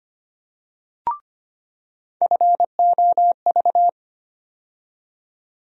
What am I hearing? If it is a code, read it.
Morse code, heard as a keyed tone, about 25 wpm, 700 Hz. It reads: FOV
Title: USA Call Sign Postfixes - 3 Letters 25wpm